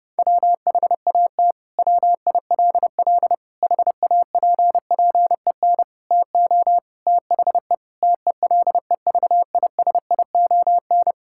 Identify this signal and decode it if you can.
Morse code, heard as a keyed tone, about 30 wpm, 700 Hz. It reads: WHAT WILL HAPPEN TO THE TELEVISION
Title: Sentences from Top 1000 Words - Review Part 5 - 30wpm